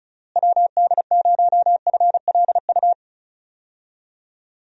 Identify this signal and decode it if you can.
Morse code, heard as a keyed tone, about 34 wpm, 700 Hz. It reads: WD0FLU
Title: USA Call Signs - 6 Characters - Repeated 3x - 35wpm